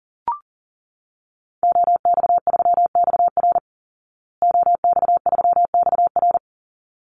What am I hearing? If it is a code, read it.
Morse code, heard as a keyed tone, about 40 wpm, 700 Hz. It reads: OX3XR OX3XR
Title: Call Signs Repeated 3x - 40wpm